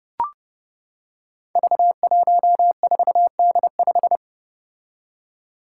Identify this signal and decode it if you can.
Morse code, heard as a keyed tone, about 31 wpm, 700 Hz. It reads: V14D5
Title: Five Challenging Letter-Number Combinations 30wpm